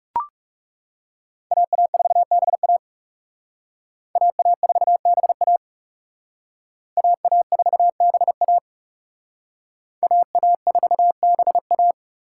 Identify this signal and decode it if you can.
Morse code, heard as a keyed tone, about 37 wpm, 700 Hz. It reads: AA4BA AA4BA AA4BA AA4BA
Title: Speed Racing - USA Call Signs 30wpm